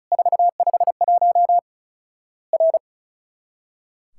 Morse code, heard as a keyed tone, about 36 wpm, 700 Hz. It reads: 451 R
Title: Worked All Europe DX Contest - QTC Exchange - 35wpm